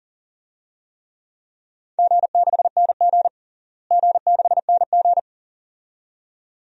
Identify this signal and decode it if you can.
Morse code, heard as a keyed tone, about 37 wpm, 700 Hz. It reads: G6NG G6NG
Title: DXCC-100 Call Signs - 4 Characters - Repeated 2x - 40wpm